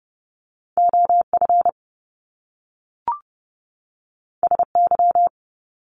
Morse code, heard as a keyed tone, about 30 wpm, 700 Hz. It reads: OF SY
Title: Top Two Letter Combinations in Words 30wpm